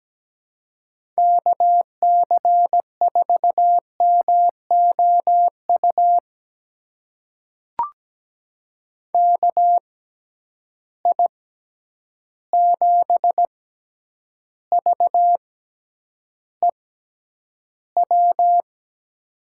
Morse code, heard as a keyed tone, about 17 wpm, spaced as for 9 wpm, 700 Hz. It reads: KC4MOU KI7VEW
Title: USA Call Signs - 6 Characters - 6x Character Spacing - 17wpm